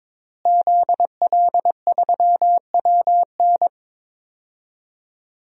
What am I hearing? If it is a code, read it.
Morse code, heard as a keyed tone, about 22 wpm, 700 Hz. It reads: ZL3WN
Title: DXCC-100 Call Signs - 5 Characters - Repeated 3x - 22wpm